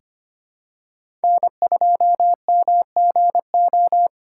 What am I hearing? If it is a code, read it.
Morse code, heard as a keyed tone, about 25 wpm, 700 Hz. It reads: N2MGO
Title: USA Call Signs - 5 Characters - 4x Character Spacing - 25wpm